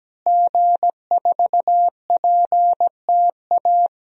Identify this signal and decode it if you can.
Morse code, heard as a keyed tone, about 17 wpm, 700 Hz. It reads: G4PTA
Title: DXCC-100 Call Signs - 5 Characters - Repeated 3x - 17wpm